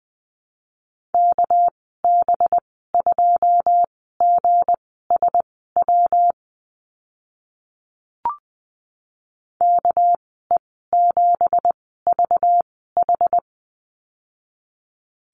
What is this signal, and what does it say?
Morse code, heard as a keyed tone, about 20 wpm, spaced as for 14 wpm, 700 Hz. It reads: KB2GSW KE7VH
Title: 213 Lesson 34g - Number 8 - Call Signs 20wpm